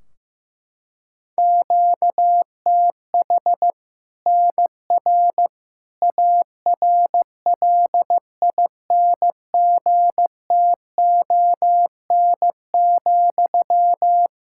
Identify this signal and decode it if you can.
Morse code, heard as a keyed tone, about 15 wpm, 700 Hz. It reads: QTH NR ARLINGTON,
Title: Practice QSO - Exchange 1 - Beginning - 15wpm